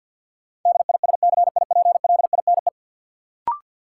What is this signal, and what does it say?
Morse code, heard as a keyed tone, about 51 wpm, 700 Hz. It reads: DISCIPLINE
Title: Sentences from Top 900 Words - New Word Drill - 50wpm